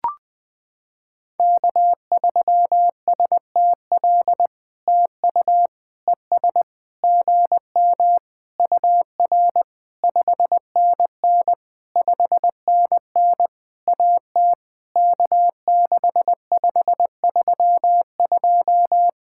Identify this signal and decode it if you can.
Morse code, heard as a keyed tone, about 20 wpm, 700 Hz. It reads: K3STL TU ES GM UR 5NN 5NN AT K6532
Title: POTA QSO Practice - 20wpm